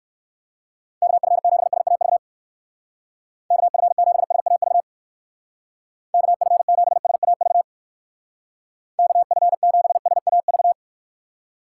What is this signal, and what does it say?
Morse code, heard as a keyed tone, about 61 wpm, 700 Hz. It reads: KF7SAV KF7SAV KF7SAV KF7SAV